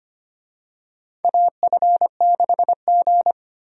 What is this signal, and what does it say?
Morse code, heard as a keyed tone, about 25 wpm, 700 Hz. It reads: AF6G
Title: USA Call Signs - 4 Characters - Repeated 2x - 25wpm